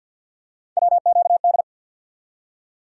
Morse code, heard as a keyed tone, about 48 wpm, 700 Hz. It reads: WQD
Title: USA Call Sign Postfixes - 3 Letters 50wpm